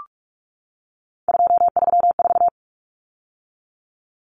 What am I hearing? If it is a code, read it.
Morse code, heard as a keyed tone, about 44 wpm, 700 Hz. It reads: 234
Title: Numbers - 3-Digit 45wpm